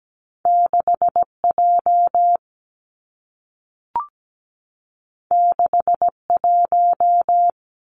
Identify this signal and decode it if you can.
Morse code, heard as a keyed tone, about 17 wpm, 700 Hz. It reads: 6 J 6 1
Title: Two Challenging Letter-Number Combinations 17wpm